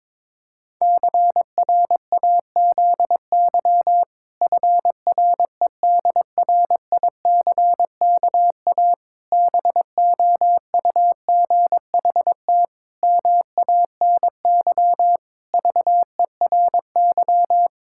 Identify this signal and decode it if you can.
Morse code, heard as a keyed tone, about 22 wpm, 700 Hz. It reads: CRAZY FREDRICKA BOUGHT MANY VERY
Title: Pangram Challenge v2 - 22wpm